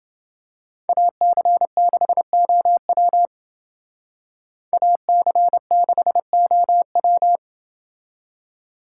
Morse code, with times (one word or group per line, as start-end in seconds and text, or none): 0.9-3.3 AC6OW
4.7-7.4 AC6OW